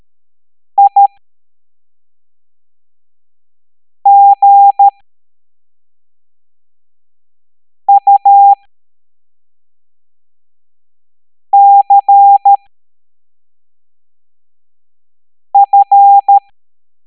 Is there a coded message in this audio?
Yes: Morse code from 0.8 to 16.4 s, heard as a keyed tone, about 13 wpm, spaced as for 3 wpm, 800 Hz. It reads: IGUCF